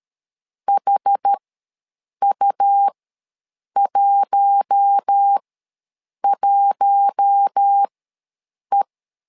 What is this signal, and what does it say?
Morse code, heard as a keyed tone, about 13 wpm, spaced as for 7 wpm, 800 Hz. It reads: HU11E
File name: LEZIONI / 4 Q1 ws 12wpm 2129